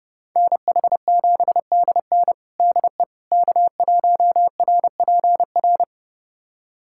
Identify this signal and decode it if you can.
Morse code, heard as a keyed tone, about 30 wpm, 700 Hz. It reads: NH7DN DE K1RPR
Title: USA Callsign DE Callsign 30wpm